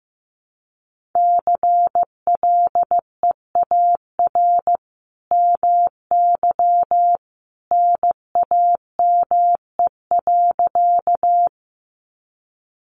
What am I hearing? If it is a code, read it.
Morse code, heard as a keyed tone, about 15 wpm, 700 Hz. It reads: CLEAR MY NAME.
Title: Sentences from Top 500 Words 15wpm